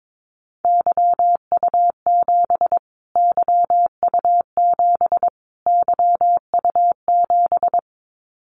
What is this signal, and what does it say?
Morse code, heard as a keyed tone, about 22 wpm, 700 Hz. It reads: YU7 YU7 YU7
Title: Callsign Prefixes for Top 25 DXCC - Repeated 3x - 22wpm